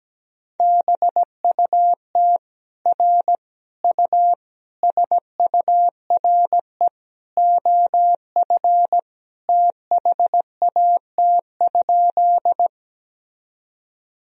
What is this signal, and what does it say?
Morse code, heard as a keyed tone, about 17 wpm, 700 Hz. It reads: BUT R U SURE OF THAT?
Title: Sentences from Top 300 Words and Abbreviations 17wpm